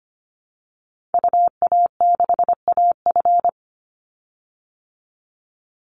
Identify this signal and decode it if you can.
Morse code, heard as a keyed tone, about 25 wpm, 700 Hz. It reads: UA6AF
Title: Call Signs Repeated 3x - 25wpm